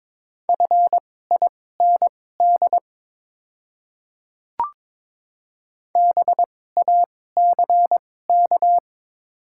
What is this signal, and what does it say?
Morse code, heard as a keyed tone, about 22 wpm, spaced as for 16 wpm, 700 Hz. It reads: FIND BACK
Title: 2x Character Spacing - Top 500 Words - 22wpm